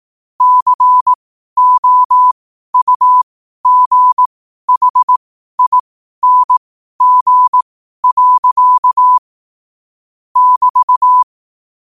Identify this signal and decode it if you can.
Morse code, heard as a keyed tone, about 18 wpm, spaced as for 12 wpm, 1000 Hz. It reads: COUGHING. <BT>
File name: qotdfiles / 12wpm